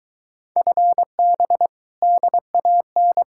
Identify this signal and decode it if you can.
Morse code, heard as a keyed tone, about 26 wpm, 700 Hz. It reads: FB DAN